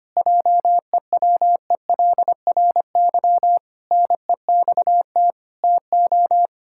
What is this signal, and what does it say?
Morse code, heard as a keyed tone, about 25 wpm, 700 Hz. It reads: JEWELRY NEXT TO
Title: Pangram Challenge v2 - 25wpm